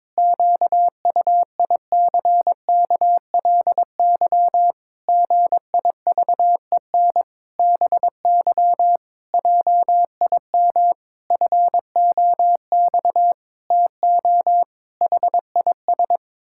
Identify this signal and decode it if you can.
Morse code, heard as a keyed tone, about 22 wpm, 700 Hz. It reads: QUICKLY GIVEN BY JIM FOX TO HIS
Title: Pangram Challenge v2 - 22wpm